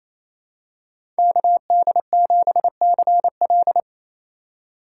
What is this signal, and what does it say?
Morse code, heard as a keyed tone, about 28 wpm, 700 Hz. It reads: KD7CL
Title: USA Call Signs - 5 Characters - 2x Character Spacing - 28wpm